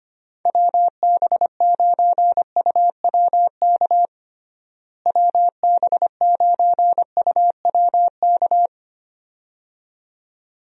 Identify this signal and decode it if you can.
Morse code, heard as a keyed tone, about 25 wpm, 700 Hz. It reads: WB9UWK WB9UWK
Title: USA Call Signs - 6 Characters - Repeated 2x - 25wpm